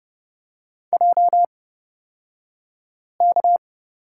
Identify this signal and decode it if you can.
Morse code, heard as a keyed tone, about 30 wpm, spaced as for 5 wpm, 700 Hz. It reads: JK
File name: Lesson 36d - BT - Two Characters 30wpm